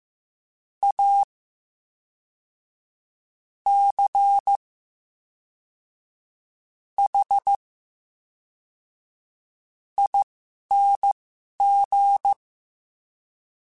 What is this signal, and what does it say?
Morse code, heard as a keyed tone, about 15 wpm, spaced as for 5 wpm, 800 Hz. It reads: ACHING
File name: Special INGs / Special INGs-mp3s / VerySlowToFaster Ing @10